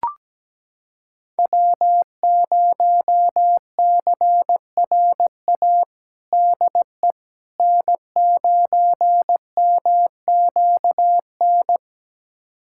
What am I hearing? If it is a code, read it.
Morse code, heard as a keyed tone, about 17 wpm, 700 Hz. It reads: W0CRA DE N9MQN
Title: USA Callsign DE Callsign - 5 Characters - 17wpm